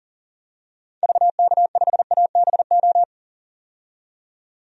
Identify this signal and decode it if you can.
Morse code, heard as a keyed tone, about 38 wpm, 700 Hz. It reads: VK5ABO
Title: DXCC-100 Call Signs - 6 Characters - 40wpm